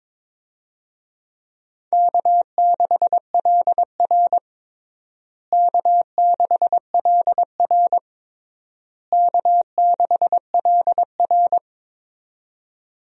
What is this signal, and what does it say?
Morse code, heard as a keyed tone, about 22 wpm, 700 Hz. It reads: K6LR K6LR K6LR
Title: USA Call Signs - 4 Characters - Repeated 3x - 22wpm